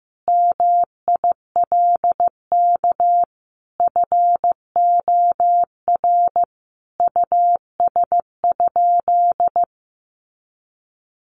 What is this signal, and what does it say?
Morse code, heard as a keyed tone, about 15 wpm, 700 Hz. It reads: MILK FOR US?
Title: Sentences from Top 1500 Words 15wpm